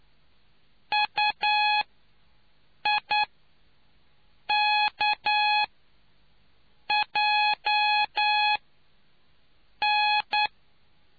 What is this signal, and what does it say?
Morse code, heard as a keyed tone, about 9 wpm, spaced as for 5 wpm, 840 Hz. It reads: UIKJN